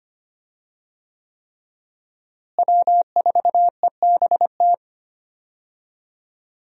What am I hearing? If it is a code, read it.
Morse code, heard as a keyed tone, about 25 wpm, 700 Hz. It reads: W4EBT
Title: USA Call Signs - 5 Characters - Repeated 3x - 25wpm